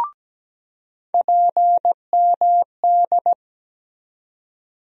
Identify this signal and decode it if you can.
Morse code, heard as a keyed tone, about 17 wpm, 700 Hz. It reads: PMD
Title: USA Call Sign Postfixes - 3 Letters 17wpm